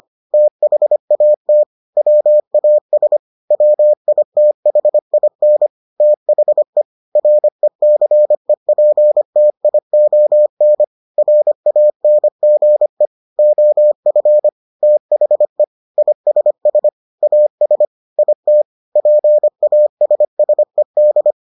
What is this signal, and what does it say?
Morse code, heard as a keyed tone, about 25 wpm, 600 Hz. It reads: THAT WAS WITHIN THE RECEPTION RANGE OF THE ISS AS IT PASSED